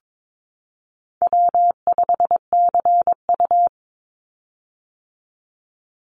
Morse code, heard as a keyed tone, about 22 wpm, 700 Hz. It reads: W5CU
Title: Easy Call Signs - 22wpm